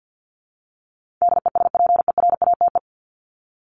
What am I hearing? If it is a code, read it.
Morse code, heard as a keyed tone, about 50 wpm, 700 Hz. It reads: DESPERATE
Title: Top 1751-2000 Words 50wpm